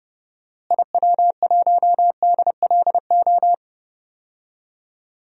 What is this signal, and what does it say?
Morse code, heard as a keyed tone, about 31 wpm, 700 Hz. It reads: IW1DLO
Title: DXCC-100 Call Signs - 6 Characters - Repeated 2x - 30wpm